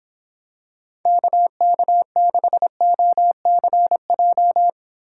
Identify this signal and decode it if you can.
Morse code, heard as a keyed tone, about 26 wpm, 700 Hz. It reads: KK6OCJ